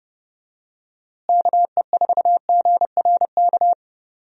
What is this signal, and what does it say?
Morse code, heard as a keyed tone, about 30 wpm, 700 Hz. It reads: KE4GRK
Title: USA Call Signs - 6 Characters - Repeated 2x - 30wpm